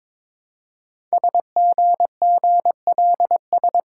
Morse code, heard as a keyed tone, about 22 wpm, 700 Hz. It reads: SGGLS
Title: Code Groups - 5 Letters - 22wpm